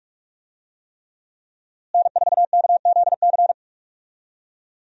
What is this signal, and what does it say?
Morse code, heard as a keyed tone, about 46 wpm, 700 Hz. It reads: N4KZC
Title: USA Call Signs - 5 Characters - Repeated 3x - 45wpm